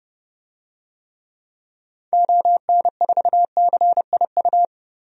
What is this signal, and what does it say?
Morse code, heard as a keyed tone, about 31 wpm, 700 Hz. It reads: ON4CIU
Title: DXCC-100 Call Signs - 6 Characters - 30wpm